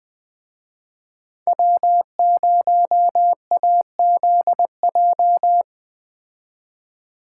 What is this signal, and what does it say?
Morse code, heard as a keyed tone, about 20 wpm, 700 Hz. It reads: W0AZJ
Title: Speed Racing - USA 1x3 Call Signs 17wpm